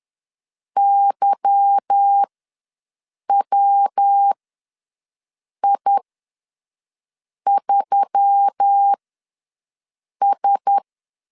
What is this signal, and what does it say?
Morse code, heard as a keyed tone, about 10 wpm, spaced as for 5 wpm, 800 Hz. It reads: YWI3S
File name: LEZIONI / 10 SIGNS / 10D COMA ws 10wpm 4314